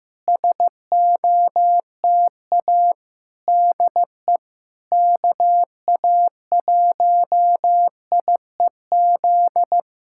Morse code, heard as a keyed tone, about 15 wpm, 700 Hz. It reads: SOTA DE KA1IEZ
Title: SOTA QSO Practice 15wpm